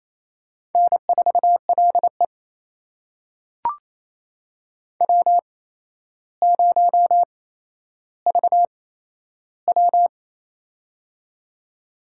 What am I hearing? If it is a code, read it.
Morse code, heard as a keyed tone, about 27 wpm, spaced as for 12 wpm, 700 Hz. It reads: N4LE W0VW
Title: USA Call Signs - 4 Characters - 8x Character Spacing - 28wpm